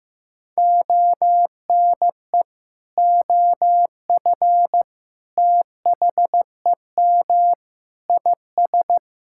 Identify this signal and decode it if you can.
Morse code, heard as a keyed tone, about 15 wpm, 700 Hz. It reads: ONE OF THEM IS
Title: Sentences from Top 1600 Words 15wpm